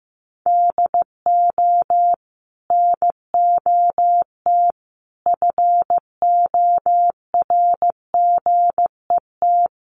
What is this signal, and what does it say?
Morse code, heard as a keyed tone, about 15 wpm, 700 Hz. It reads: DO NOT FORGET
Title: Sentences from Top 1200 Words 15wpm